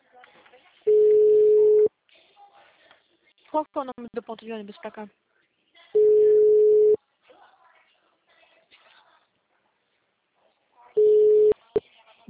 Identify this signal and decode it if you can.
Ringback tone (Europe, 425 Hz), with speech.
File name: out- -2009- - - .28285